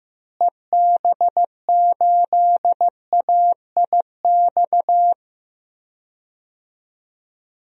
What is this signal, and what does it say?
Morse code, heard as a keyed tone, about 15 wpm, 700 Hz. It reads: EB8AIX
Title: DXCC-100 Call Signs - 6 Characters - Repeated 3x - 15wpm